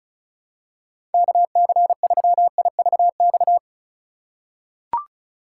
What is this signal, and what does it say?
Morse code, heard as a keyed tone, about 34 wpm, 700 Hz. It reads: KC3IVX
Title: USA Call Signs - 6 Characters - 2x Character Spacing - 35wpm